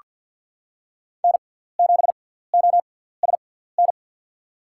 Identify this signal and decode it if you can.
Morse code, heard as a keyed tone, about 47 wpm, spaced as for 17 wpm, 700 Hz. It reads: N7OSN
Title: USA Call Signs - 5 Characters - 6x Character Spacing - 50wpm